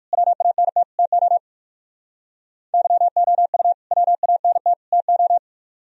Morse code, heard as a keyed tone, about 45 wpm, 700 Hz. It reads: WANT TO YOU WANT TO
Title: Speed-Racing - Sets of 3 Words 45wpm